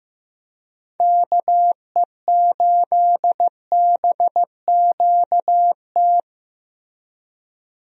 Morse code, heard as a keyed tone, about 15 wpm, 700 Hz. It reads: KE8BQT
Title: USA Call Signs - 6 Characters - Repeated 2x - 15wpm